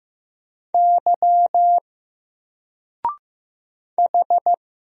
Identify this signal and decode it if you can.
Morse code, heard as a keyed tone, about 15 wpm, 700 Hz. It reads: Y H
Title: Single Character (Letter:Number:Punctuation:Prosign) Alt 15wpm